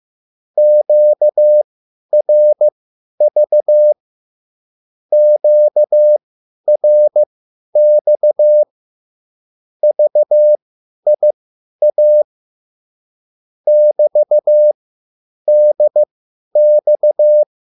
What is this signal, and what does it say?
Morse code, heard as a keyed tone, about 15 wpm, spaced as for 10 wpm, 600 Hz. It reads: QRV QRX VIA <BT>DX
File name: QSL / CW lesson